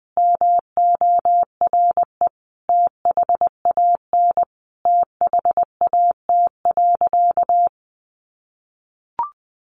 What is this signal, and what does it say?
Morse code, heard as a keyed tone, about 20 wpm, 700 Hz. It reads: MORE THAN THAT.
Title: Sentences from Top 800 Words 20wpm